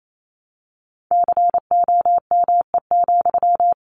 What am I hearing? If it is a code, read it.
Morse code, heard as a keyed tone, about 28 wpm, 700 Hz. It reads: COME,
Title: Sentences from Top 1200 Words 28wpm